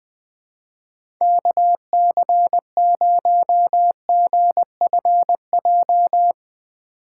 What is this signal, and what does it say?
Morse code, heard as a keyed tone, about 20 wpm, 700 Hz. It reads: KC0GFJ
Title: USA Call Signs - 6 Characters - 8x Character Spacing - 20wpm